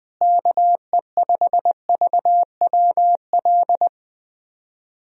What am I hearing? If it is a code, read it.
Morse code, heard as a keyed tone, about 20 wpm, 700 Hz. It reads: KE5VWL
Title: USA Call Signs - 6 Characters - Repeated 2x - 20wpm